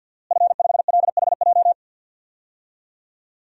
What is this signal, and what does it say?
Morse code, heard as a keyed tone, about 49 wpm, 700 Hz. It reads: F5LHJ
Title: DXCC-100 Call Signs - 5 Characters - Repeated 3x - 50wpm